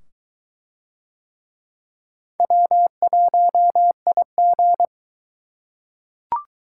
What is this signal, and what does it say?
Morse code, heard as a keyed tone, about 22 wpm, 700 Hz. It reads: W1IG